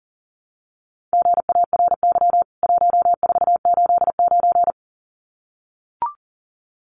Morse code, heard as a keyed tone, about 40 wpm, 700 Hz. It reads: GARY 1489
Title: CWT Exchange 40wpm